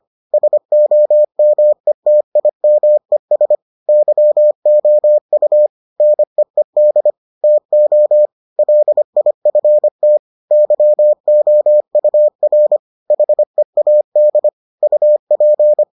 Morse code, heard as a keyed tone, about 25 wpm, 600 Hz. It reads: SOMETIMES YOU NEED TO LIFT YOUR HEAD UP